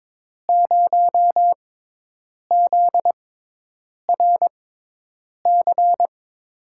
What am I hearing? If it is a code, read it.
Morse code, heard as a keyed tone, about 23 wpm, spaced as for 8 wpm, 700 Hz. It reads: 0ZRC
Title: USA Call Signs - 6 Characters - 6x Character Spacing - 22wpm